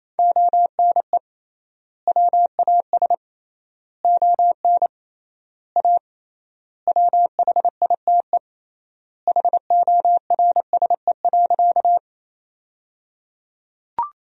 Morse code, heard as a keyed tone, about 28 wpm, 700 Hz. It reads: ONE WAS ON A WHITE HORSE.
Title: 3x Word-Spacing - Sentences from Top 300 Words - 28wpm